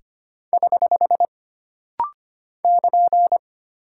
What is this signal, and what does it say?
Morse code, heard as a keyed tone, about 25 wpm, 700 Hz. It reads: <HH> <KN>
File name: Single Punctuation-Prosign - Rapid-Fire - 25wpm